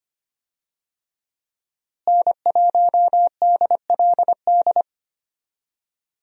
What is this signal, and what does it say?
Morse code, heard as a keyed tone, about 25 wpm, 700 Hz. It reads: N1DLD
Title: USA Call Signs - 5 Characters - Repeated 2x - 25wpm